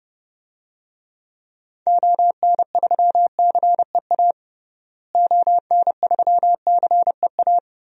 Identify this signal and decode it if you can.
Morse code, heard as a keyed tone, about 30 wpm, 700 Hz. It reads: ON3CEA ON3CEA
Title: DXCC-100 Call Signs - 6 Characters - Repeated 3x - 30wpm